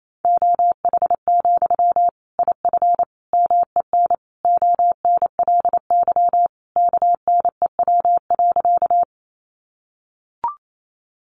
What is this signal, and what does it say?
Morse code, heard as a keyed tone, about 28 wpm, 700 Hz. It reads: OH, IF MEN ONLY KNEW.
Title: Sentences from Top 500 Words 28wpm